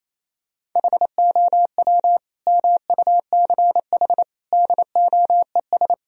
Morse code, heard as a keyed tone, about 28 wpm, 700 Hz. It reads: HOW MUCH DOES
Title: Sentences from Top 1000 Words - Review Part 4 - 28wpm